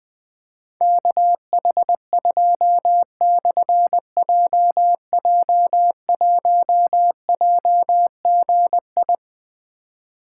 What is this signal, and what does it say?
Morse code, heard as a keyed tone, about 20 wpm, 700 Hz. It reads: KH2/JJ1JGI
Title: Difficult Call Signs 20wpm